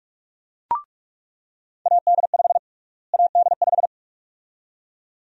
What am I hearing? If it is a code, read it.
Morse code, heard as a keyed tone, about 47 wpm, 700 Hz. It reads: AD5 AD5
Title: USA Call Sign Prefixes - Repeated 2x - 45wpm